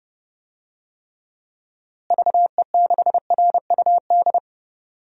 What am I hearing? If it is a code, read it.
Morse code, heard as a keyed tone, about 30 wpm, 700 Hz. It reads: VE6RUD
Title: DXCC-100 Call Signs - 6 Characters - Repeated 3x - 30wpm